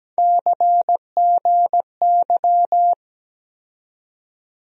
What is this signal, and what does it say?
Morse code, heard as a keyed tone, about 17 wpm, 700 Hz. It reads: CGY
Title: USA Call Sign Postfixes - 3 Letters 17wpm